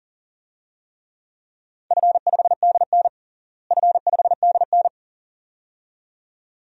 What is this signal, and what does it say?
Morse code, heard as a keyed tone, about 40 wpm, 700 Hz. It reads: F5DN F5DN
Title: DXCC-100 Call Signs - 4 Characters - Repeated 2x - 40wpm